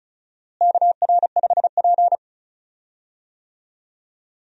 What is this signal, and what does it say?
Morse code, heard as a keyed tone, about 36 wpm, 700 Hz. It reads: KR5P
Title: USA Call Signs - 4 Characters - Repeated 3x - 35wpm